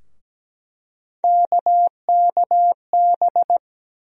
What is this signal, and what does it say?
Morse code, heard as a keyed tone, about 17 wpm, 700 Hz. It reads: KKB
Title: USA Call Sign Postfixes - 3 Letters 17wpm